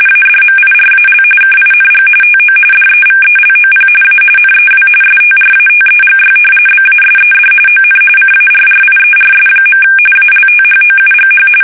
RTTY, about 45.45 baud. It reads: ADMIRAL KING TO DEVELOP A NEW STRATEGY BASED ON CARRIERS. ALTHOUGH
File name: fsk850